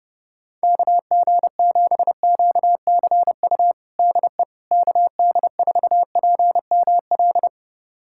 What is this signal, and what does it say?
Morse code, heard as a keyed tone, about 30 wpm, 700 Hz. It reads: KG7QCU DE KD4PML
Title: USA Callsign DE Callsign - 6 Characters - 30wpm